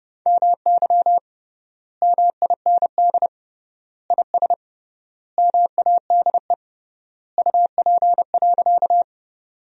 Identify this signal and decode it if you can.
Morse code, heard as a keyed tone, about 30 wpm, 700 Hz. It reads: MY MIND IS MADE UP.
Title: 3x Word-Spacing - Sentences from Top 500 Words - 30wpm